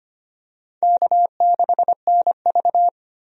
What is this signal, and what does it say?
Morse code, heard as a keyed tone, about 25 wpm, 700 Hz. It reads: K6NV
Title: USA Call Signs - 4 Characters - 25wpm